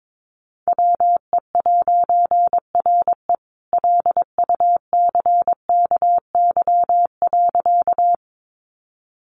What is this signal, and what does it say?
Morse code, heard as a keyed tone, about 22 wpm, 700 Hz. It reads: WE'RE LUCKY.
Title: A Sound Of Thunder 22wpm